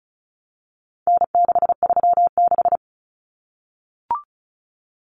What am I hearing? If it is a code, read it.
Morse code, heard as a keyed tone, about 34 wpm, 700 Hz. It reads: N636
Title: Common Cut Numbers - 4-Digit 35wpm